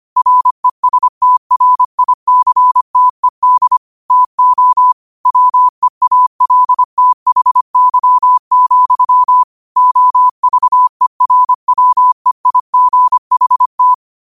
Morse code, heard as a keyed tone, about 25 wpm, 1000 Hz. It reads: RESTRICTED TO WEALTHY, OVERWEIGHT
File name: qotdfiles / 25wpm